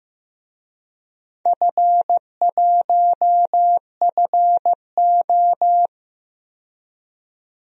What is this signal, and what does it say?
Morse code, heard as a keyed tone, about 15 wpm, 700 Hz. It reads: F1FO
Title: DXCC-100 Call Signs - 4 Characters - 15wpm